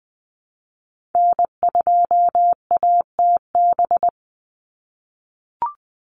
Morse code, heard as a keyed tone, about 20 wpm, 700 Hz. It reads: N2ATB
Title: Call Signs - 20wpm